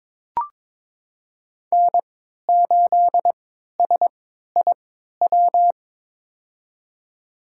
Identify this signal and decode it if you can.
Morse code, heard as a keyed tone, about 23 wpm, spaced as for 13 wpm, 700 Hz. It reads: N8SIW
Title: USA Call Signs - 5 Characters - 3x Character Spacing - 22wpm